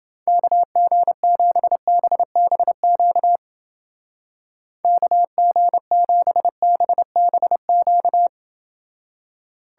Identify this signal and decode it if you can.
Morse code, heard as a keyed tone, about 29 wpm, 700 Hz. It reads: KG7BBQ KG7BBQ